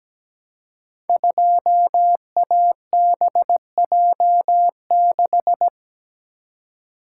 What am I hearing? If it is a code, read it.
Morse code, heard as a keyed tone, about 17 wpm, 700 Hz. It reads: 2ABJ6
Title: Code Groups - 5 Letter:Numbers - 17wpm